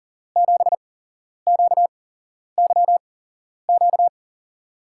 Morse code, heard as a keyed tone, about 42 wpm, spaced as for 11 wpm, 700 Hz. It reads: 7QYQ
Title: USA Call Signs - 6 Characters - 8x Character Spacing - 40wpm